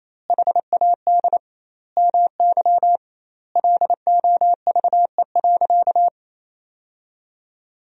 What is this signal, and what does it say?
Morse code, heard as a keyed tone, about 28 wpm, 700 Hz. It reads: HAD MY LOVE.
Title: 2x Word-Spacing - Sentences from Top 400 Words - 28wpm